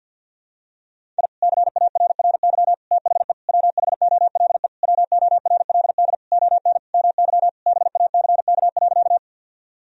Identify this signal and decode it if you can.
Morse code, heard as a keyed tone, about 52 wpm, 700 Hz. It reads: I CARRY THE WHOLE WORLD ON MY BACK.
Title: Sentences from Top 1000 Words - Review Part 1 - 50wpm